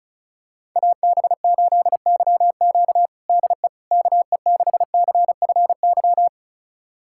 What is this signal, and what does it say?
Morse code, heard as a keyed tone, about 36 wpm, 700 Hz. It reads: AB8YQ DE KE6CFY
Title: USA Callsign DE Callsign 35wpm